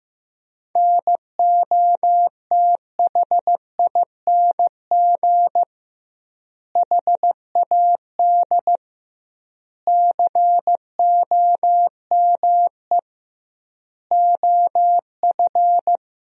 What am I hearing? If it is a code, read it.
Morse code, heard as a keyed tone, about 16 wpm, 700 Hz. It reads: NOTHING HAD COME OF
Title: 2x Word-Spacing - Sentences from Top 500 Words - 15wpm